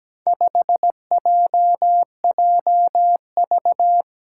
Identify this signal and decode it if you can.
Morse code, heard as a keyed tone, about 17 wpm, 700 Hz. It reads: 5JJV
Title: Four Challenging Letter-Number Combinations 17wpm